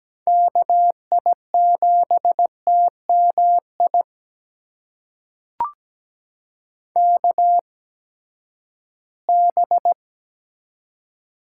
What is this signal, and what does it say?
Morse code, heard as a keyed tone, about 17 wpm, spaced as for 12 wpm, 700 Hz. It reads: KI7TMI KB
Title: USA Call Signs - 6 Characters - 8x Character Spacing - 17wpm